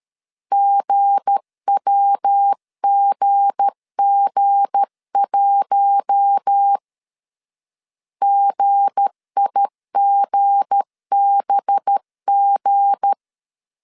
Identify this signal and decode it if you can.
Morse code, heard as a keyed tone, about 13 wpm, 800 Hz. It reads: GWGG1 GIGBG
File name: LEZIONI / 7 GF9Z / 7 G 12wpm 2055